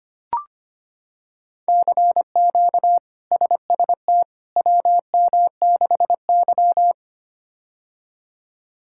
Morse code, heard as a keyed tone, about 25 wpm, 700 Hz. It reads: CQ SST WM6Y
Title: K1USN SST QSO Practice - 25wpm